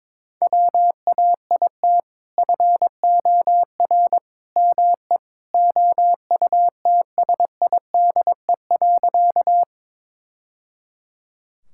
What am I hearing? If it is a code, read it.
Morse code, heard as a keyed tone, about 22 wpm, 700 Hz. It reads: WAIT FOR ME OUTSIDE.
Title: Sentences from Top 1000 Words - Review Part 6 - 22wpm